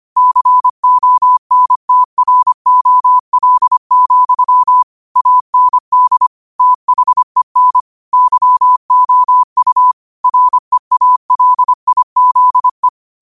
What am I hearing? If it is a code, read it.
Morse code, heard as a keyed tone, about 25 wpm, 1000 Hz. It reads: CONTROL, AND THEN YOU REALIZE